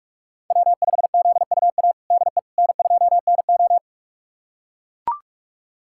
Morse code, heard as a keyed tone, about 41 wpm, 700 Hz. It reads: W5ZUA DE N2NO
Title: USA Callsign DE Callsign 45wpm